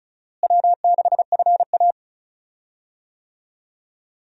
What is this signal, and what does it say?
Morse code, heard as a keyed tone, about 34 wpm, 700 Hz. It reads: W6FA
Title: USA Call Signs - 4 Characters - Repeated 3x - 35wpm